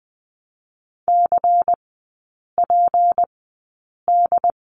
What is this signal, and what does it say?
Morse code, heard as a keyed tone, about 20 wpm, spaced as for 8 wpm, 700 Hz. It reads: CPD
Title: 152 Lesson 25e - Letter p - Three Characters 20wpm